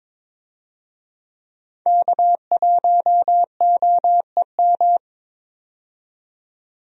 Morse code, heard as a keyed tone, about 21 wpm, 700 Hz. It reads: K1OEM
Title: USA Call Signs - 5 Characters - 22wpm